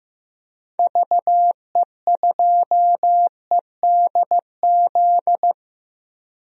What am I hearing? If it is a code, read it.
Morse code, heard as a keyed tone, about 15 wpm, 700 Hz. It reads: VE2EDZ
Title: DXCC-100 Call Signs - 6 Characters - Repeated 3x - 15wpm